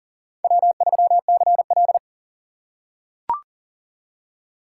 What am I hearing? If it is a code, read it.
Morse code, heard as a keyed tone, about 40 wpm, 700 Hz. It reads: W3CL
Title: USA Call Signs - 4 Characters - 3x Character Spacing - 40wpm